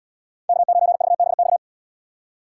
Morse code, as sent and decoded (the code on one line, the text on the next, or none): -.- ----. ...- -... -..-
K9VBX